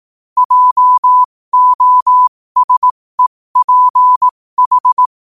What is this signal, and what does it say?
Morse code, heard as a keyed tone, about 18 wpm, spaced as for 15 wpm, 1000 Hz. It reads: JOSEPH
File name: qotdfiles / 15wpm